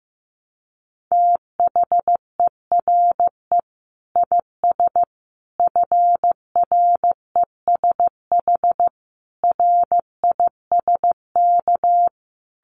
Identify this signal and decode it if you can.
Morse code, heard as a keyed tone, about 15 wpm, 700 Hz. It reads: THERE IS FRESH RISK
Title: Sentences from Top 1200 Words 15wpm